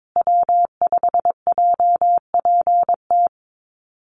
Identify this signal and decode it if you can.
Morse code, heard as a keyed tone, about 22 wpm, 700 Hz. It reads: W5JPT
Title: USA Call Signs Repeated 2x 22wpm